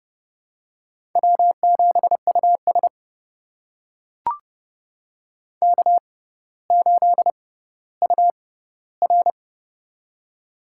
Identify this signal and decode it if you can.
Morse code, heard as a keyed tone, about 30 wpm, spaced as for 15 wpm, 700 Hz. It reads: W7US K8UR
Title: USA Call Signs - 4 Characters - 6x Character Spacing - 30wpm